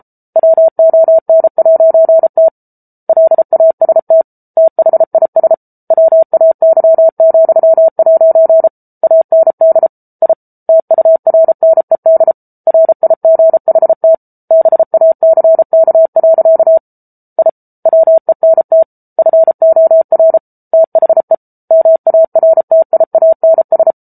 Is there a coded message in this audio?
Yes: Morse code from 0.4 to 23.9 s, heard as a keyed tone, about 34 wpm, 650 Hz. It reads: WON'T LAST THIS WAY,' AND I TURNED RIGHT BACK. I WENT FOR THE MARTIANS